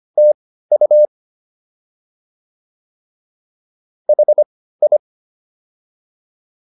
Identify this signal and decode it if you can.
Morse code, heard as a keyed tone, about 25 wpm, spaced as for 15 wpm, 600 Hz. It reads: TU HI